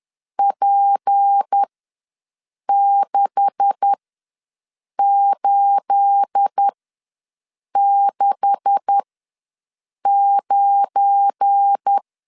Morse code, heard as a keyed tone, about 10 wpm, spaced as for 6 wpm, 800 Hz. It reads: P6869
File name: LEZIONI / 9 286 ws 10wpm 4032